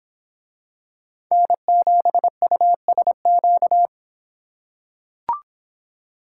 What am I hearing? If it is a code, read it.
Morse code, heard as a keyed tone, about 26 wpm, 700 Hz. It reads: N7USQ